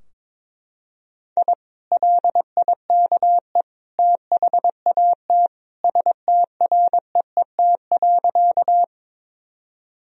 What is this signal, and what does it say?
Morse code, heard as a keyed tone, about 22 wpm, 700 Hz. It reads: I LIKE THAT STREET.
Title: Sentences from Top 1000 Words - Review Part 3 - 22wpm